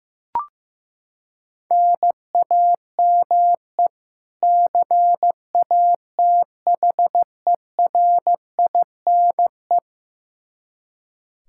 Morse code, heard as a keyed tone, about 16 wpm, 700 Hz. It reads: NAME CATHERINE
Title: Practice QSO - Exchange 1 - Beginning - 15wpm